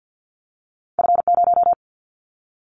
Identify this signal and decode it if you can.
Morse code, heard as a keyed tone, about 50 wpm, 700 Hz. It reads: F0
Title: Callsign Prefixes for Top 50 DXCC - 50wpm